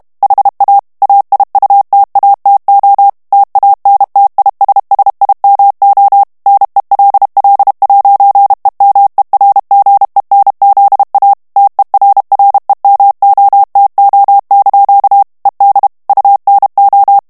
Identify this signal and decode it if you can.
Morse code, heard as a keyed tone, about 32 wpm, 800 Hz. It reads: HA AIUTATO TANTISSIMO NELL'EMERGENZA TERREMOTO) ED UNO
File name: ITALIEN texte clair 1 vitesse 30 mots minute